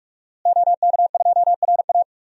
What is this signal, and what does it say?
Morse code, heard as a keyed tone, about 46 wpm, 700 Hz. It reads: OK2RA
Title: DXCC-100 Call Signs - 5 Characters - Repeated 2x - 45wpm